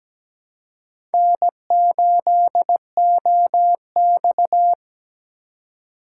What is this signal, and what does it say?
Morse code, heard as a keyed tone, about 17 wpm, 700 Hz. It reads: N8OX
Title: USA Call Signs - 4 Characters - 2x Character Spacing - 17wpm